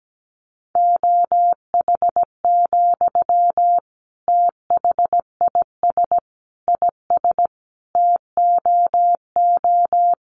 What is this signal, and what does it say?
Morse code, heard as a keyed tone, about 17 wpm, 700 Hz. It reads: OH, THIS IS TOO
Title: Sentences from Top 500 Words 17wpm